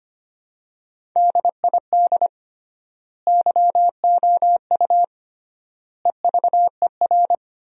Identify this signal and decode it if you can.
Morse code, heard as a keyed tone, about 25 wpm, 700 Hz. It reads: DID YOU EVER
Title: 3x Word-Spacing - Sentences from Top 400 Words - 25wpm